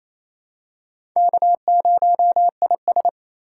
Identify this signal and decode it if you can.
Morse code, heard as a keyed tone, about 28 wpm, 700 Hz. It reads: K0IS
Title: USA Call Signs - 3x Character Spacing - 28wpm